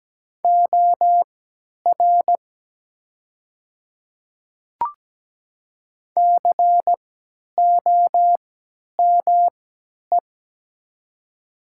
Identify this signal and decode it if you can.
Morse code, heard as a keyed tone, about 17 wpm, spaced as for 10 wpm, 700 Hz. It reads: OR COME
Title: 3x Character Spacing - Top 100 Words - 17wpm